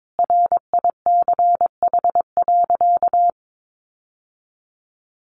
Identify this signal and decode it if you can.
Morse code, heard as a keyed tone, about 22 wpm, 700 Hz. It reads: RICH.
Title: Sentences from Top 1100 Words 22wpm